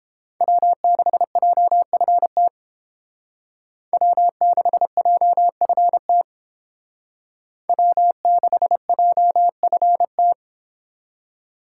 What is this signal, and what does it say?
Morse code, heard as a keyed tone, about 30 wpm, 700 Hz. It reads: W6JFT W6JFT W6JFT